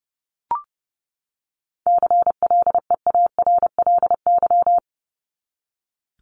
Morse code, heard as a keyed tone, about 28 wpm, 700 Hz. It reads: CLEARLY
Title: Top 1001-1250 Words 30wpm